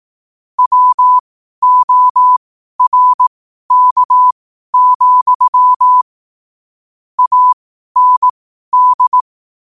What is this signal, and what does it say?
Morse code, heard as a keyed tone, about 18 wpm, spaced as for 12 wpm, 1000 Hz. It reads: WORK, AND
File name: qotdfiles / 12wpm